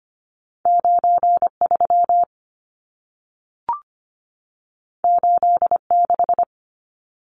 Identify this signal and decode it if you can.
Morse code, heard as a keyed tone, about 25 wpm, 700 Hz. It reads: 93 86
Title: Numbers - 2-Digit 25wpm